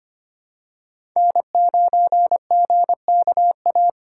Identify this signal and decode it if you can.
Morse code, heard as a keyed tone, about 25 wpm, 700 Hz. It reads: N9GKA
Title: USA Call Signs - 5 Characters - 4x Character Spacing - 25wpm